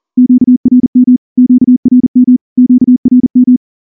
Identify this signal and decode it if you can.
Morse code, heard as a keyed tone, about 38 wpm, 250 Hz. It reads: QRM QRM QRM
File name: morse / morse-training-40WPM